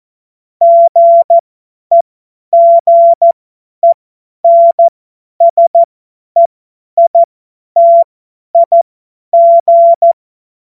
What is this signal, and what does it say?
Morse code, heard as a keyed tone, about 14 wpm, spaced as for 10 wpm, 680 Hz. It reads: GEGENSEITIG
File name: Morsekurs / Witze / Witz 22. Juni 2021, 50 F70